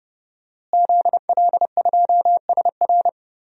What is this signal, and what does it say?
Morse code, heard as a keyed tone, about 30 wpm, 700 Hz. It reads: ZL2SR
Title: DXCC-100 Call Signs - 5 Characters - Repeated 3x - 30wpm